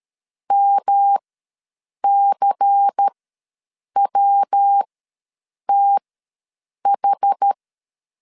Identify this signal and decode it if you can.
Morse code, heard as a keyed tone, about 13 wpm, spaced as for 7 wpm, 800 Hz. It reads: MCWTH
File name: LEZIONI / 6 PKX0 / 6 PK ws 12wpm 3116